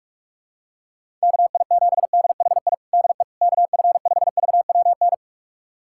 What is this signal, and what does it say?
Morse code, heard as a keyed tone, about 46 wpm, 690 Hz. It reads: KI7DHI DE KF5VWN
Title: USA Callsign DE Callsign - 6 Characters - 45wpm